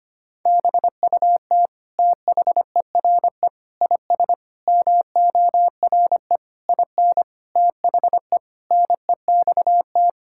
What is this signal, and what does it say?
Morse code, heard as a keyed tone, about 25 wpm, 700 Hz. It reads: BUT THERE IS MORE IN THE NEXT
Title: Sentences from Top 1000 Words - Review Part 5 - 25wpm